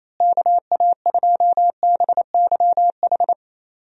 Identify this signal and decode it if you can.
Morse code, heard as a keyed tone, about 28 wpm, 700 Hz. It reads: KA2BYH
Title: USA Call Signs - 6 Characters - Repeated 2x - 28wpm